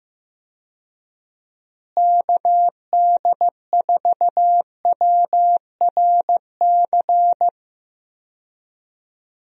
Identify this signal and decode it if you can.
Morse code, heard as a keyed tone, about 15 wpm, 700 Hz. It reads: KD4WRC
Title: USA Call Signs - 6 Characters - Repeated 3x - 15wpm